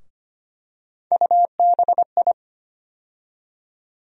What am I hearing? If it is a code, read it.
Morse code, heard as a keyed tone, about 25 wpm, 700 Hz. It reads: UBI
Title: USA Call Sign Postfixes - 3 Letters 25wpm